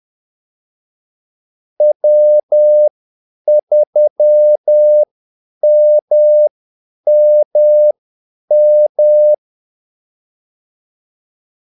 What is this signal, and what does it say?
Morse code, heard as a keyed tone, about 10 wpm, spaced as for 8 wpm, 600 Hz. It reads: W3MMM
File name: Callsigns-10-8wpm